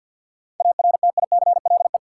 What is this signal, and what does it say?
Morse code, heard as a keyed tone, about 52 wpm, 700 Hz. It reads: ARTICLE
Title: Sentences from Top 1000 Words - Review Part 5 - New Word Drill - 50wpm